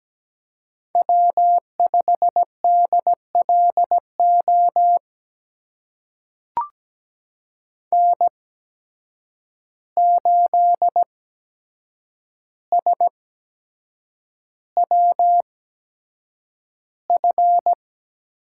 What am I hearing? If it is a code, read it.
Morse code, heard as a keyed tone, about 17 wpm, spaced as for 7 wpm, 700 Hz. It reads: W5DLO N8SWF
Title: USA Call Signs - 5 Characters - 8x Character Spacing - 17wpm